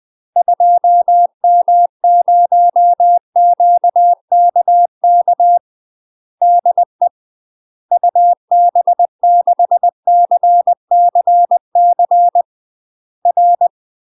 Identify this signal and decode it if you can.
Morse code, heard as a keyed tone, about 20 wpm, 700 Hz. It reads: 2M0QKK DE UB6CCC R
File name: UK MASTER QSOs TRAINING / 20WPM / UK MASTER QSOs 0017